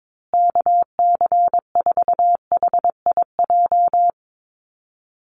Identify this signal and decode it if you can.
Morse code, heard as a keyed tone, about 22 wpm, 700 Hz. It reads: KC4HIJ
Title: USA Call Signs Repeated 2x 22wpm